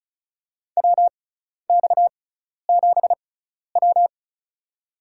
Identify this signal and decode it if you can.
Morse code, heard as a keyed tone, about 34 wpm, spaced as for 12 wpm, 700 Hz. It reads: WX7W
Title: USA Call Signs - 4 Characters - 6x Character Spacing - 35wpm